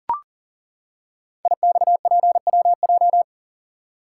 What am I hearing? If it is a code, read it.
Morse code, heard as a keyed tone, about 38 wpm, 700 Hz. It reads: IXPWJ
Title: Code Groups - 5 Letters - 40wpm